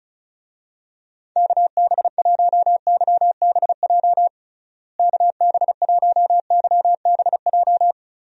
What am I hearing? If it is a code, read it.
Morse code, heard as a keyed tone, about 36 wpm, 700 Hz. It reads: KB1YBJ KB1YBJ
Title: USA Call Signs - 6 Characters - Repeated 2x - 35wpm